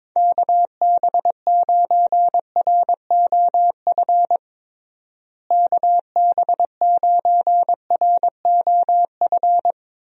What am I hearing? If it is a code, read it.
Morse code, heard as a keyed tone, about 22 wpm, 700 Hz. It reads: KB9ROF KB9ROF
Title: USA Call Signs - 6 Characters - Repeated 3x - 22wpm